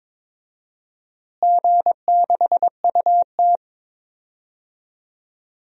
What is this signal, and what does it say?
Morse code, heard as a keyed tone, about 22 wpm, 700 Hz. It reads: G6UT
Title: DXCC-100 Call Signs - 4 Characters - Repeated 3x - 22wpm